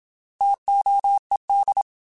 Morse code, heard as a keyed tone, about 27 wpm, 800 Hz. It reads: TOED
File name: T Prefixes / T mp3s